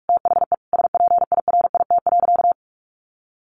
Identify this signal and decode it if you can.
Morse code, heard as a keyed tone, about 44 wpm, 700 Hz. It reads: THE SPIRIT.
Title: Sentences from Top 1400 Words 45wpm